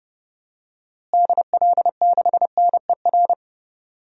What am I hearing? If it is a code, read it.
Morse code, heard as a keyed tone, about 30 wpm, 700 Hz. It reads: DL6NER
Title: DXCC-100 Call Signs - 6 Characters - Repeated 3x - 30wpm